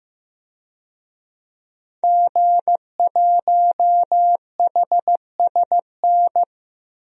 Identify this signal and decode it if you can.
Morse code, heard as a keyed tone, about 15 wpm, 700 Hz. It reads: G1HSN
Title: DXCC-100 Call Signs - 5 Characters - 15wpm